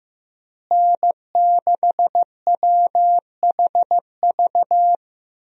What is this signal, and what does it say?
Morse code, heard as a keyed tone, about 16 wpm, 700 Hz. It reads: N6WHV
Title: USA Call Signs - 5 Characters - Repeated 2x - 15wpm